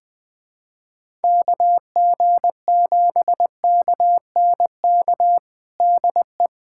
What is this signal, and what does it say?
Morse code, heard as a keyed tone, about 20 wpm, 700 Hz. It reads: KG7KNK DE
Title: USA Callsign DE Callsign - 6 Characters - 20wpm